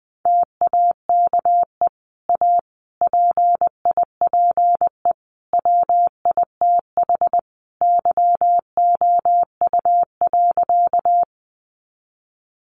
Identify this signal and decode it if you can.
Morse code, heard as a keyed tone, about 20 wpm, 700 Hz. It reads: TAKE A PIPE WITH YOU.
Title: Sentences from Top 1500 Words 20wpm